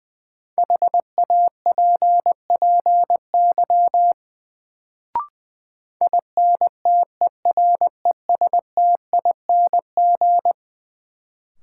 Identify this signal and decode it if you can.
Morse code, heard as a keyed tone, about 20 wpm, 700 Hz. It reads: HAPPY INTERESTING
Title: Sentences from Top 1000 Words - New Word Drill - 20wpm